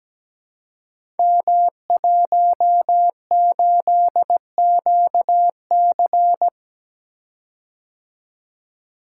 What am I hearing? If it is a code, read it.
Morse code, heard as a keyed tone, about 17 wpm, 700 Hz. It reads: M18QC
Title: Code Groups - 5 Letter:Numbers - 17wpm